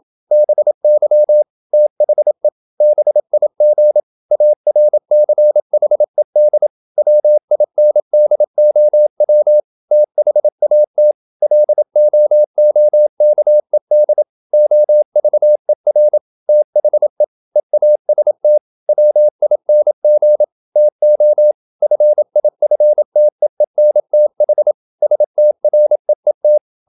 Morse code, heard as a keyed tone, about 27 wpm, 600 Hz. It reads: BY THE BIG ARCHED WINDOW THAT LOOKED OVER THE EAST WING TO FIFTEENTH STREET
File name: Never 0027wpm